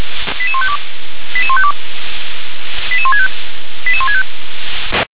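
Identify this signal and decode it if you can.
Five-tone selective call (ZVEI), heard as quick runs of five tones, tones of about 70 ms. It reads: 88152 88152 88165 88165
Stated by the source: Five-tone ZVEI call 88160 received on an SDR.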